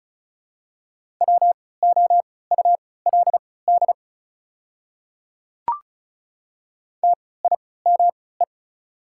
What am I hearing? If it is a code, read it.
Morse code, heard as a keyed tone, about 36 wpm, spaced as for 20 wpm, 690 Hz. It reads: WOULD TIME
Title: 3x Character Spacing - Top 100 Words - 35wpm